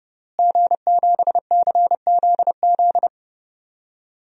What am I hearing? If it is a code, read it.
Morse code, heard as a keyed tone, about 32 wpm, 700 Hz. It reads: G7CZZ
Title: DXCC-100 Call Signs - 5 Characters - Repeated 2x - 30wpm